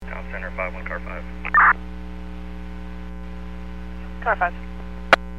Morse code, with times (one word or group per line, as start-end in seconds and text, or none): none